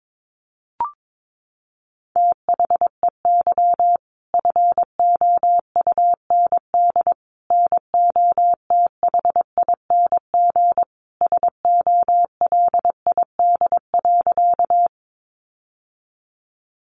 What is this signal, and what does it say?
Morse code, heard as a keyed tone, about 22 wpm, 700 Hz. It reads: THEY FOUND NOTHING SOLID.
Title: Sentences from Top 1500 Words 22wpm